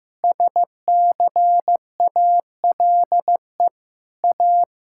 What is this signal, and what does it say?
Morse code, heard as a keyed tone, about 15 wpm, 700 Hz. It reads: SCALE A
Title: Sentences from Top 1000 Words - Review Part 5 - 15wpm